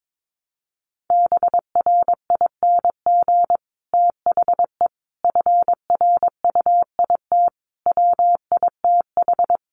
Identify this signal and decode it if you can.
Morse code, heard as a keyed tone, about 22 wpm, 700 Hz. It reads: BRING THE FRUIT WITH
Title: Sentences from Top 1300 Words 22wpm